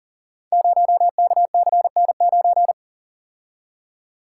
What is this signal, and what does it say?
Morse code, heard as a keyed tone, about 38 wpm, 700 Hz. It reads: 0KCN9
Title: Code Groups - 5 Letter:Numbers - 40wpm